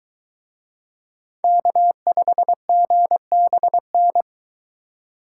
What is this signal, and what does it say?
Morse code, heard as a keyed tone, about 23 wpm, 700 Hz. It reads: K5GBN